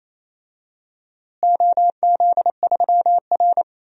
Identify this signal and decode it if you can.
Morse code, heard as a keyed tone, about 28 wpm, 700 Hz. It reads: OZ3R
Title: DXCC-100 Call Signs - 4 Characters - Repeated 2x - 28wpm